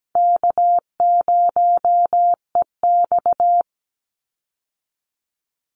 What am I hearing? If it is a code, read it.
Morse code, heard as a keyed tone, about 17 wpm, 700 Hz. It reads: K0EX
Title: Easy Call Signs Repeated 2x - 17wpm